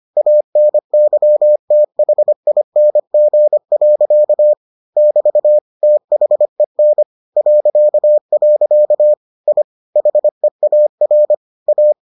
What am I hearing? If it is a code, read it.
Morse code, heard as a keyed tone, about 25 wpm, 600 Hz. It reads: ANYTHING. <BT> THEN .. I HEAR A